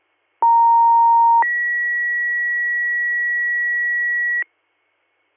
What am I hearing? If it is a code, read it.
Two-tone page, heard as a one-second tone, then a three-second tone, about 930 then 1950 Hz. It is heard from 0.4 to 4.4 s.